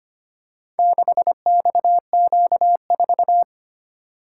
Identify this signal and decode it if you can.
Morse code, heard as a keyed tone, about 25 wpm, 700 Hz. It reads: 6XQ4
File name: practice / 25wpm / Four Letter-Number Combinations - 25wpm